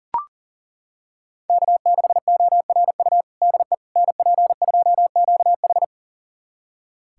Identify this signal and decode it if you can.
Morse code, heard as a keyed tone, about 40 wpm, 700 Hz. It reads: K6ORU DE NP2QH
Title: USA Callsign DE Callsign - 5 Characters - 40wpm